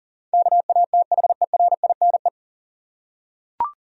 Morse code, heard as a keyed tone, about 37 wpm, 700 Hz. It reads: KATHERINE
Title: USA Common Names - 40wpm